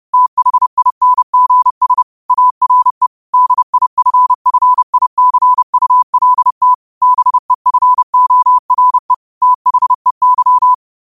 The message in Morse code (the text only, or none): THINGS ARE DIFFICULT BEFORE THEY